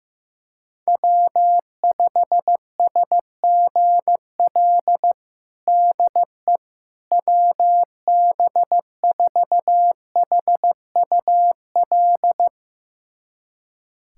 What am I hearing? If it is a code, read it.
Morse code, heard as a keyed tone, about 15 wpm, 700 Hz. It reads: W5SGL DE WB4HUL
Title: USA Callsign DE Callsign 15wpm